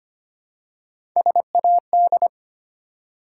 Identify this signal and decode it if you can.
Morse code, heard as a keyed tone, about 25 wpm, 700 Hz. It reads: SAD
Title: Three Letter Words - 25wpm